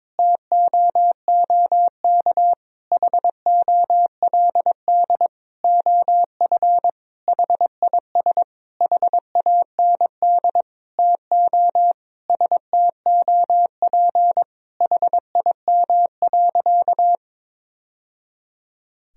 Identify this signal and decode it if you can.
Morse code, heard as a keyed tone, about 22 wpm, 700 Hz. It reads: TOOK HOLD OF HIS HAND TO STOP HIM.
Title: Sentences from Top 1000 Words - Review Part 4 - 22wpm